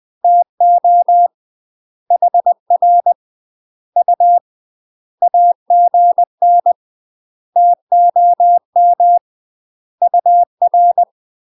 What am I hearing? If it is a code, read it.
Morse code, heard as a keyed tone, about 20 wpm, 700 Hz. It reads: TO HR U AGN TOM UR
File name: UK MASTER QSOs TRAINING / 20WPM / UK MASTER QSOs 0026